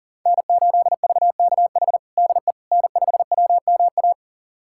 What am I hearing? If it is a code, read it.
Morse code, heard as a keyed tone, about 40 wpm, 700 Hz. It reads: N8VKH DE N5WMA
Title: USA Callsign DE Callsign - 5 Characters - 40wpm